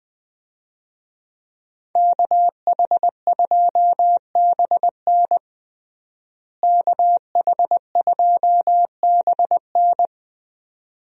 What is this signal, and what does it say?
Morse code, heard as a keyed tone, about 21 wpm, 700 Hz. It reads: KH2BN KH2BN
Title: USA Call Signs - 5 Characters - Repeated 2x - 20wpm